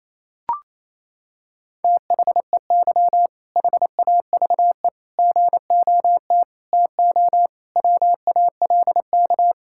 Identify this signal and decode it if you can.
Morse code, heard as a keyed tone, about 28 wpm, 700 Hz. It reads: THEY HAVE GOT TO WALK
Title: Sentences from Top 1000 Words - Review Part 6 - 28wpm